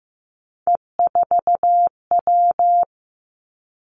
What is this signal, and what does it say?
Morse code, heard as a keyed tone, about 15 wpm, 700 Hz. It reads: E4W
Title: Three Challenging Letter-Number Combinations 15wpm